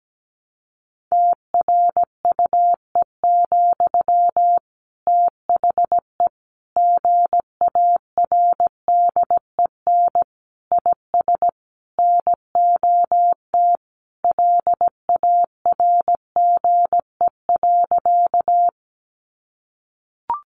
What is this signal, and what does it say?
Morse code, heard as a keyed tone, about 17 wpm, 700 Hz. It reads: TRUE, THE GARDEN IS NOT LARGE.
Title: Sentences from Top 700 Words 17wpm